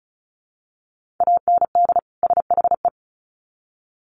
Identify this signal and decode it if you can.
Morse code, heard as a keyed tone, about 34 wpm, 700 Hz. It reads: AND SHE
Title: Sets of 2 Words 35wpm